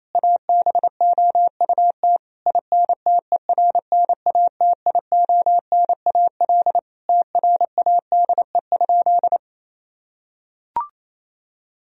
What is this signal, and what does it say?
Morse code, heard as a keyed tone, about 28 wpm, 700 Hz. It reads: ABOUT INTERNATIONAL TRADE?
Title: Sentences from Top 1000 Words - Review Part 4 - 28wpm